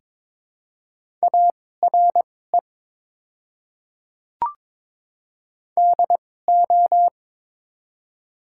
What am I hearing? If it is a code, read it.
Morse code, heard as a keyed tone, about 22 wpm, spaced as for 16 wpm, 700 Hz. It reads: ARE DO
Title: 2x Character Spacing - Top 100 Words - 22wpm